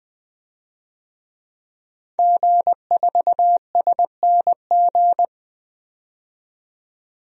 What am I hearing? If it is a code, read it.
Morse code, heard as a keyed tone, about 20 wpm, 700 Hz. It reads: G4SNG
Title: DXCC-100 Call Signs - 5 Characters - 20wpm